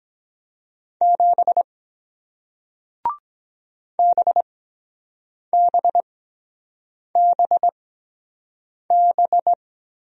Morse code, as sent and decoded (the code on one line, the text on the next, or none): --... / -... -... -... -...
7 BBBB